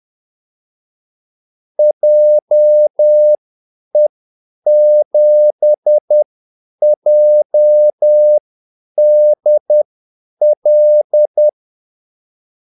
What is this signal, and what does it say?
Morse code, heard as a keyed tone, about 10 wpm, spaced as for 8 wpm, 600 Hz. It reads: JE7JDL
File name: Callsigns-10-8wpm